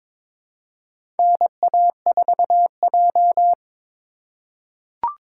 Morse code, heard as a keyed tone, about 22 wpm, 700 Hz. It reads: NA4J
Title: USA Call Signs - 3x Character Spacing - 22wpm